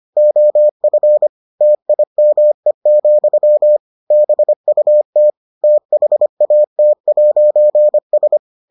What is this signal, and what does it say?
Morse code, heard as a keyed tone, about 25 wpm, 600 Hz. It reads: OF TIME, BUT THAT'S